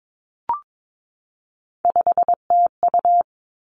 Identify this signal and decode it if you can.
Morse code, heard as a keyed tone, about 22 wpm, 700 Hz. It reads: 5TU
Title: Three Challenging Letter-Number Combinations 22wpm